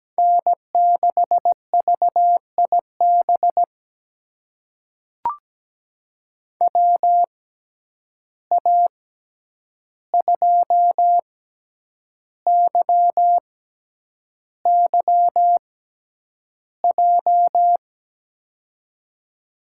Morse code, heard as a keyed tone, about 17 wpm, spaced as for 8 wpm, 700 Hz. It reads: N6VIB WA2YYJ
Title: USA Call Signs - 6x Character Spacing - 17wpm